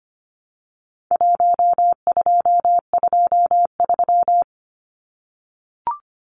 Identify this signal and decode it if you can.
Morse code, heard as a keyed tone, about 25 wpm, 700 Hz. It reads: 1223
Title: Numbers - Contest Style - 4-Digit 25wpm